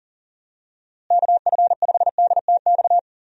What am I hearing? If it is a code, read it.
Morse code, heard as a keyed tone, about 40 wpm, 700 Hz. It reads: KF5DTX
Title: USA Call Signs - 6 Characters - Repeated 3x - 40wpm